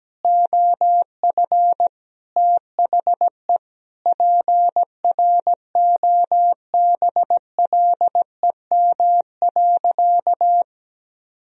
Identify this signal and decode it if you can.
Morse code, heard as a keyed tone, about 17 wpm, 700 Hz. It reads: OF THE PROBLEM.
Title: Sentences from Top 1000 Words - Review Part 2 - 17wpm